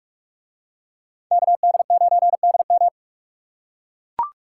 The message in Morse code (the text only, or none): KD9DM